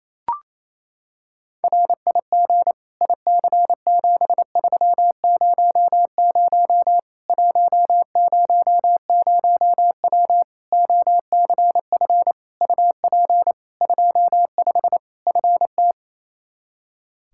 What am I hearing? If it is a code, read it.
Morse code, heard as a keyed tone, about 28 wpm, 700 Hz. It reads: RIG IC7300 100W OCF UP 25 FT
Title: Practice QSO - Exchange 2 - Info - 28wpm